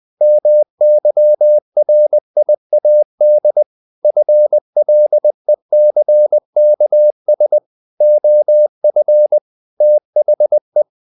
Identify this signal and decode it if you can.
Morse code, heard as a keyed tone, about 20 wpm, 600 Hz. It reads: MYRIAD FLECKS OF THE
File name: the old man and the sea / the old man and the sea part 1 20wpm